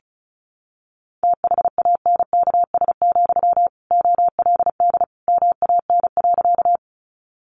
Morse code, heard as a keyed tone, about 34 wpm, 700 Hz. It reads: THANKS, OLD MAN.
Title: Sentences from Top 600 Words 35wpm